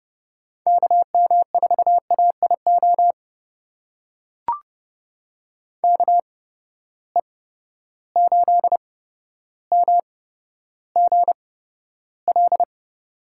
Morse code, heard as a keyed tone, about 31 wpm, spaced as for 13 wpm, 700 Hz. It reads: KM4AIO KE8MGL
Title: USA Call Signs - 6 Characters - 8x Character Spacing - 30wpm